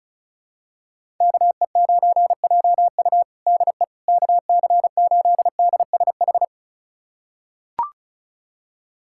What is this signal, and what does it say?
Morse code, heard as a keyed tone, about 34 wpm, 700 Hz. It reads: KE9JU DE KC8DSH
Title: USA Callsign DE Callsign 35wpm